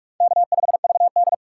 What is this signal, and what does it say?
Morse code, heard as a keyed tone, about 41 wpm, 700 Hz. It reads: K5VD
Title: USA Call Signs - 3x Character Spacing - 45wpm